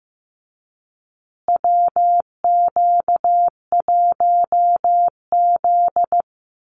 Morse code, heard as a keyed tone, about 15 wpm, 700 Hz. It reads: WQ1Z
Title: USA Call Signs Repeated 2x 15wpm